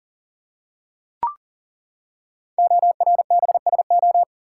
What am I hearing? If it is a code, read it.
Morse code, heard as a keyed tone, about 38 wpm, 700 Hz. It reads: ORBSO
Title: Code Groups - 5 Letter:Numbers - 40wpm